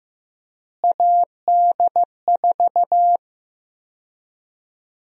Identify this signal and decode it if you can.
Morse code, heard as a keyed tone, about 15 wpm, 700 Hz. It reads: AD4
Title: USA Call Sign Prefixes - Repeated 2x - 15wpm